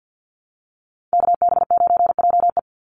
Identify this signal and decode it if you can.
Morse code, heard as a keyed tone, about 50 wpm, 700 Hz. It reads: KB9JE
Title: USA Call Signs 50wpm